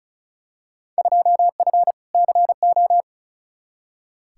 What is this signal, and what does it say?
Morse code, heard as a keyed tone, about 36 wpm, 700 Hz. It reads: 2F CO
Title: Field Day Exchanges - 35wpm